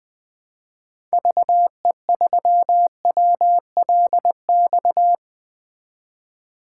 Morse code, heard as a keyed tone, about 20 wpm, 700 Hz. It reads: VE3WLX
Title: 2023 POTA Activator Call Signs - 20wpm